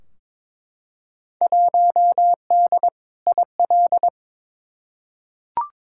Morse code, heard as a keyed tone, about 22 wpm, 700 Hz. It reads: 1D IL
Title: Field Day Exchanges - 22wpm